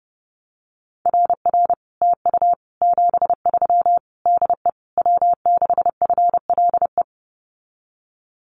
Morse code, heard as a keyed tone, about 30 wpm, 700 Hz. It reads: RR TU 73 DE W6FLE
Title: Common QSO Phrases Encore 30wpm